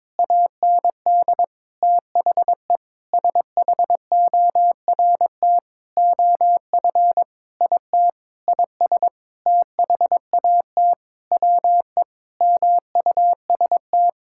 Morse code, heard as a keyed tone, about 22 wpm, 700 Hz. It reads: AND THE SHORT OF IT IS THAT WE MUST
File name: Sentences with 9-Word Phrases - 22wpm